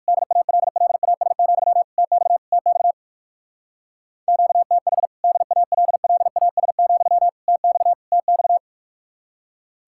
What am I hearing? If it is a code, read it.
Morse code, heard as a keyed tone, about 54 wpm, 700 Hz. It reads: DALLAS, TX TX QTH DALLAS, TX TX
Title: Speed-Racing Common QSO Phrases - 45wpm